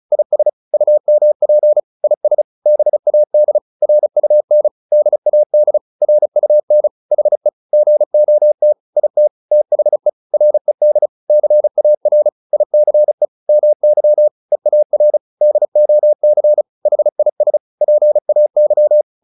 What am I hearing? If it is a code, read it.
Morse code, heard as a keyed tone, about 34 wpm, 600 Hz. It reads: IS UMP IS BAD RUN DAD RUN HE GOT IT THE RED CAR ICE MY EAR DOC HIS PAY